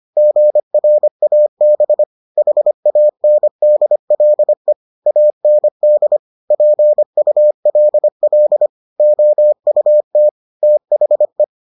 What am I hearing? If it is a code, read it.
Morse code, heard as a keyed tone, about 25 wpm, 600 Hz. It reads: GRAB HANDLE AND PULL OUT THE